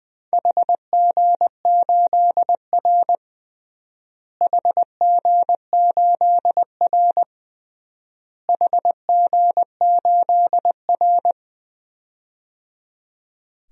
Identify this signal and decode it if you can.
Morse code, heard as a keyed tone, about 20 wpm, 700 Hz. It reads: HG8R HG8R HG8R
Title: DXCC-100 Call Signs - 4 Characters - Repeated 3x - 20wpm